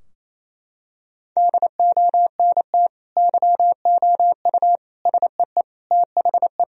Morse code, heard as a keyed tone, about 28 wpm, 700 Hz. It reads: DONT YOU SEE THE
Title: Sentences from Top 1000 Words - Review Part 2 - 28wpm